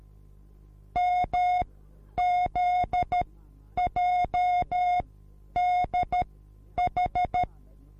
Morse code, heard as a keyed tone, about 13 wpm, spaced as for 9 wpm, 720 Hz. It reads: MZJDH